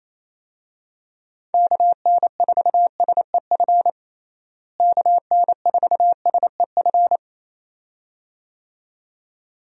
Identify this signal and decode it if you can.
Morse code, heard as a keyed tone, about 28 wpm, 700 Hz. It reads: KN4SEF KN4SEF
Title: USA Call Signs - 6 Characters - Repeated 2x - 28wpm